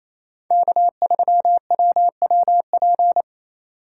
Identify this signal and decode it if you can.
Morse code, heard as a keyed tone, about 28 wpm, 700 Hz. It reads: K3WWP
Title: K1USN SST QSO Practice - 28wpm